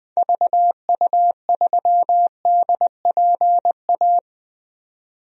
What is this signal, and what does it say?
Morse code, heard as a keyed tone, about 20 wpm, 700 Hz. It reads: VU3DPA
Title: DXCC-100 Call Signs - 6 Characters - 20wpm